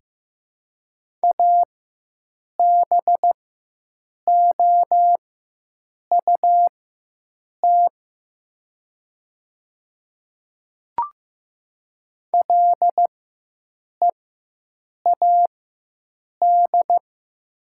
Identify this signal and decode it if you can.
Morse code, heard as a keyed tone, about 15 wpm, spaced as for 7 wpm, 700 Hz. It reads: ABOUT LEAD
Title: 4x Character Spacing - Top 500 Words - 15wpm